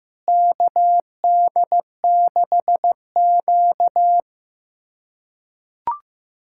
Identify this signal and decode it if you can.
Morse code, heard as a keyed tone, about 15 wpm, 700 Hz. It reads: KD6Q
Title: USA Callsign DE Callsign - 4 Characters - 15wpm